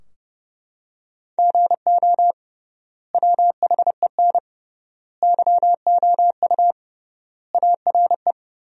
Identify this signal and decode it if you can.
Morse code, heard as a keyed tone, about 30 wpm, 700 Hz. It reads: GO WHEN YOU ARE
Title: 3x Word-Spacing - Sentences from Top 500 Words - 30wpm